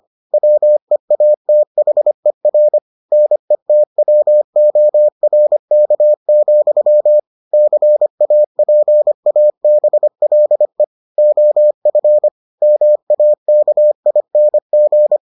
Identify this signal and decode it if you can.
Morse code, heard as a keyed tone, about 25 wpm, 600 Hz. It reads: WEATHER NETWORK, CAPABLE OF MAKING